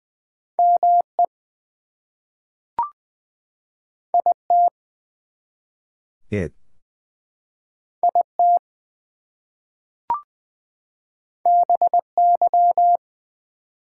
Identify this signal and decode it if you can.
Morse code, heard as a keyed tone, about 20 wpm, 700 Hz. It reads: ME IT IT BY
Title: Two Letter Words - 20wpm